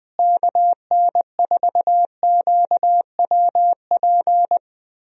Morse code, heard as a keyed tone, about 20 wpm, 700 Hz. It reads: KN4QWP
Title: USA Call Signs - 6 Characters - Repeated 3x - 20wpm